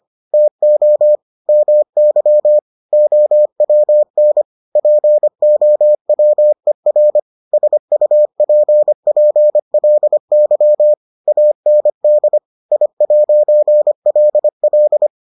Morse code, heard as a keyed tone, about 25 wpm, 600 Hz. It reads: TO MY OWN POWER SUPPLY AND I'LL